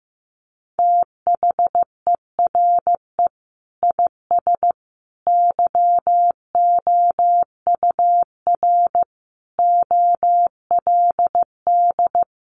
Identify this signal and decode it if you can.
Morse code, heard as a keyed tone, about 15 wpm, 700 Hz. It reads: THERE IS YOUR OLD
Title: Sentences from Top 500 Words 15wpm